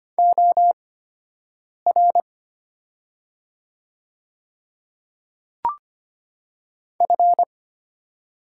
Morse code, heard as a keyed tone, about 25 wpm, spaced as for 7 wpm, 700 Hz. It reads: OR F
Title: 8x Character Spacing - Top 200 Words - 25wpm